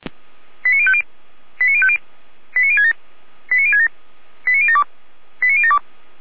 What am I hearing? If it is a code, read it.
Five-tone selective call (ZVEI), heard as quick runs of five tones, tones of about 70 ms. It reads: 89050 89050 89067 89067 89072 89072
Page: /455/SDR/ZVEI/89031/